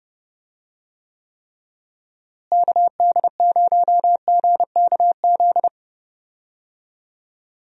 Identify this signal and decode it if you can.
Morse code, heard as a keyed tone, about 32 wpm, 700 Hz. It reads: KD0GKZ
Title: USA Call Signs - 6 Characters - Repeated 3x - 30wpm